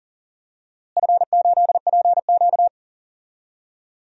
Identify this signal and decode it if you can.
Morse code, heard as a keyed tone, about 39 wpm, 700 Hz. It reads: F8PQ
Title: DXCC-100 Call Signs - 4 Characters - Repeated 3x - 40wpm